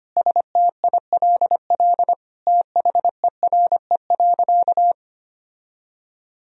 Morse code, heard as a keyed tone, about 25 wpm, 700 Hz. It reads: STILL THERE.
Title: Sentences from Top 1000 Words - Review Part 1 - 25wpm